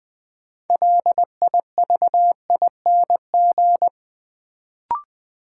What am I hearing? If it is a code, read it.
Morse code, heard as a keyed tone, about 20 wpm, 700 Hz. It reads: LIVING
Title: Sentences from Top 1000 Words - Review Part 6 - New Word Drill - 20wpm